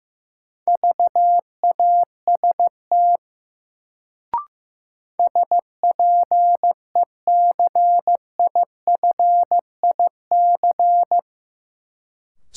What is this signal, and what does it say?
Morse code, heard as a keyed tone, about 15 wpm, 700 Hz. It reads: VAST SPECIFIC
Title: Sentences from Top 1300 Words - New Word Drill Alt - 15wpm